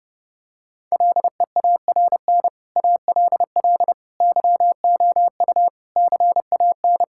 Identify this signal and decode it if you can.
Morse code, heard as a keyed tone, about 32 wpm, 700 Hz. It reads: LEARN ALL YOU CAN
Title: Sentences from Top 1000 Words - Review Part 5 - 30wpm